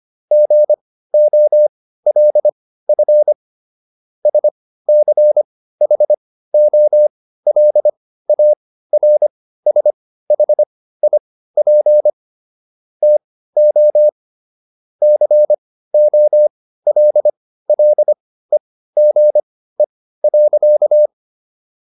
Morse code, heard as a keyed tone, about 25 wpm, spaced as for 15 wpm, 600 Hz. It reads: GOLF SCHOLARSHIP TO COLLEGE.